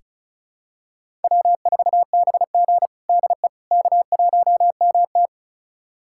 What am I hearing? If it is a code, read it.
Morse code, heard as a keyed tone, about 34 wpm, 700 Hz. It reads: W4BG DE K1MT
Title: USA Callsign DE Callsign - 4 Characters - 35wpm